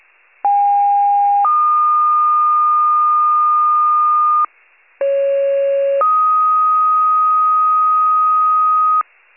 Two-tone page, heard as a one-second tone, then a three-second tone, about 810 then 1220 Hz, about 550 then 1250 Hz. It is heard from 0.4 to 9.0 s.